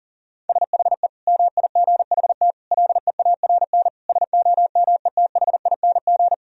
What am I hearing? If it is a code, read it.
Morse code, heard as a keyed tone, about 37 wpm, 690 Hz. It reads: SHE MIGHT LEARN SOMETHING
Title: Sentences from Top 1000 Words - Review Part 4 - 40wpm